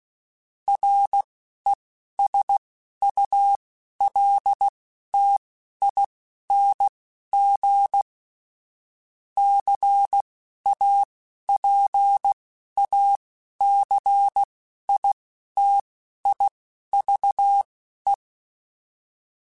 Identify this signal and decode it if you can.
Morse code, heard as a keyed tone, about 16 wpm, spaced as for 11 wpm, 800 Hz. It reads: RESULTING CAPACITIVE
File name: Technical Subjects / Technical Subjects-mp3s / XC @10